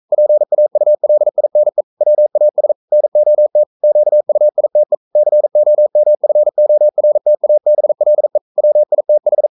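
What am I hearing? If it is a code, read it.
Morse code, heard as a keyed tone, about 42 wpm, 600 Hz. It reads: PAULINE WAS NOT QUITE COMFORTABLE WITH
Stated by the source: Morse code sending of an English novel at 42 wpm.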